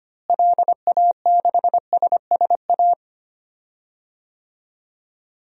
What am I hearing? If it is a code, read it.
Morse code, heard as a keyed tone, about 25 wpm, 700 Hz. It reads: LA6SSA
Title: DXCC-100 Call Signs - 6 Characters - Repeated 2x - 25wpm